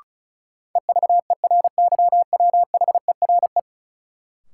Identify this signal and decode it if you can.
Morse code, heard as a keyed tone, about 36 wpm, 700 Hz. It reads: EVERYWHERE
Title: Sentences from Top 1600 Words - New Word Drill Alt - 35wpm